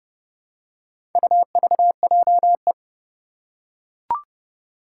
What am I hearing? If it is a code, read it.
Morse code, heard as a keyed tone, about 31 wpm, 700 Hz. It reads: UVJE
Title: Four Challenging Letter-Number Combinations 30wpm